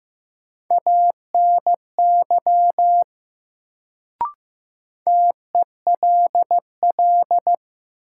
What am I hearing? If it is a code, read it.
Morse code, heard as a keyed tone, about 15 wpm, 700 Hz. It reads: ANY TELL
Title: Sentences from Top 200 Words - New Word Drill - 15wpm